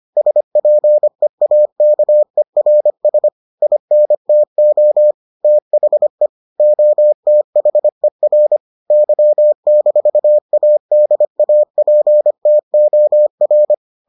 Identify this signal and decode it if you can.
Morse code, heard as a keyed tone, about 25 wpm, 600 Hz. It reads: SPEAKERS INTO THE OTHER Y-ADAPTOR